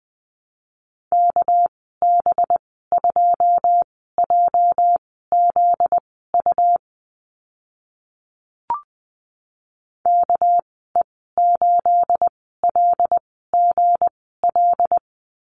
Morse code, heard as a keyed tone, about 20 wpm, spaced as for 14 wpm, 700 Hz. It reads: KB2JZU KE8LGL
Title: 245 Lesson 39g - Letter z - Call Signs 20wpm